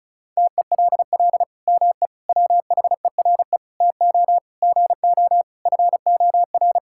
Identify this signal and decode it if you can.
Morse code, heard as a keyed tone, about 34 wpm, 700 Hz. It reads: TELL ME WHERE TO GO FOR
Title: Sentences from Top 1600 Words 35wpm